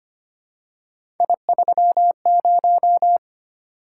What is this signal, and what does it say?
Morse code, heard as a keyed tone, about 25 wpm, 700 Hz. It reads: I30
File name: Three Letter-Number Combinations - 25wpm